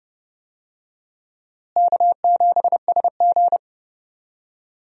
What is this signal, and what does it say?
Morse code, heard as a keyed tone, about 30 wpm, 700 Hz. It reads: K7SG
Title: USA Call Signs - 4 Characters - Repeated 3x - 30wpm